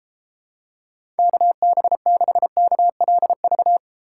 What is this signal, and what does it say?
Morse code, heard as a keyed tone, about 33 wpm, 700 Hz. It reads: KB6KLV